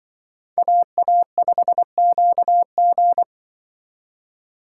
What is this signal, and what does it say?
Morse code, heard as a keyed tone, about 24 wpm, 700 Hz. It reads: AA5QG
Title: Speed Racing - USA Call Signs 20wpm